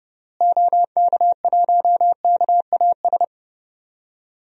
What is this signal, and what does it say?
Morse code, heard as a keyed tone, about 30 wpm, 700 Hz. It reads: OK1KAS
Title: DXCC-100 Call Signs - 6 Characters - Repeated 3x - 30wpm